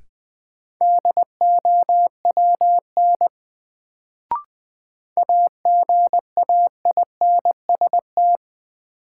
Morse code, heard as a keyed tone, about 20 wpm, 700 Hz. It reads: DOWN AGAINST
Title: Sets of 2 Words - New Word Drill - 20wpm